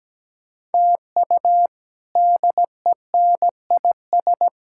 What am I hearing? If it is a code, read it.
Morse code, heard as a keyed tone, about 17 wpm, 700 Hz. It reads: TU DENIS
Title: K1USN SST QSO Practice - 17wpm